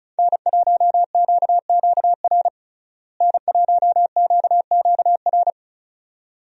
Morse code, heard as a keyed tone, about 34 wpm, 700 Hz. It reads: N1QQR N1QQR
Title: USA Call Signs - 5 Characters - Repeated 2x - 35wpm